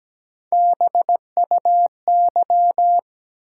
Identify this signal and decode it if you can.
Morse code, heard as a keyed tone, about 17 wpm, 700 Hz. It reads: BUY
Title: Sentences from Top 700 Words - New Word Drill - 17wpm